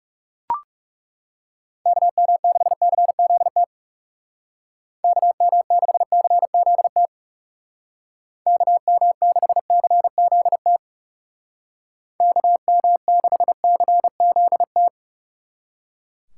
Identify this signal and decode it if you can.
Morse code, heard as a keyed tone, about 36 wpm, 700 Hz. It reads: KM6CZT KM6CZT KM6CZT KM6CZT
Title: Speed Racing - USA 2x3 Call Signs 30wpm